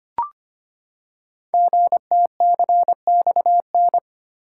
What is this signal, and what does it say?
Morse code, heard as a keyed tone, about 25 wpm, 700 Hz. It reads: GTCXN
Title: Code Groups - 5 Letters - 25wpm